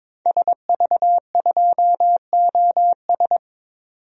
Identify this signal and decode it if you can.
Morse code, heard as a keyed tone, about 22 wpm, 700 Hz. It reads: SV2OS
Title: DXCC-100 Call Signs - 5 Characters - 22wpm